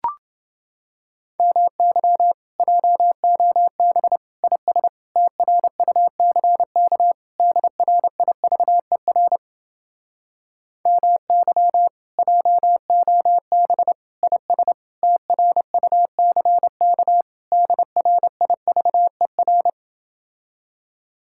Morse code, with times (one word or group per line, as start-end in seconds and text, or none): 1.4-2.3 MY
2.6-4.2 JOB
4.4-4.9 IS
5.2-7.1 TRUCK
7.4-9.4 DRIVER
10.9-11.9 MY
12.2-13.9 JOB
14.2-14.7 IS
15.0-17.2 TRUCK
17.5-19.7 DRIVER